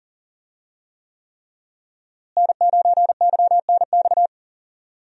Morse code, heard as a keyed tone, about 40 wpm, 700 Hz. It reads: N9YNX
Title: USA Call Signs - 5 Characters - Repeated 3x - 40wpm